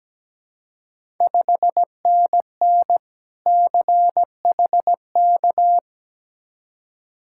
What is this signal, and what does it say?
Morse code, heard as a keyed tone, about 17 wpm, 700 Hz. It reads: 5NN CHK
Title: Ontario QSO Party Exchange 17wpm